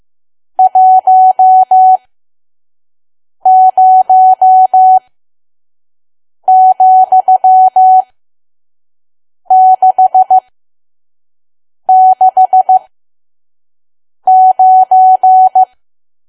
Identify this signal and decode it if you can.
Morse code, heard as a keyed tone, about 15 wpm, spaced as for 5 wpm, 750 Hz. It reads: 10,669
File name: Archive / 5 WPM files / PM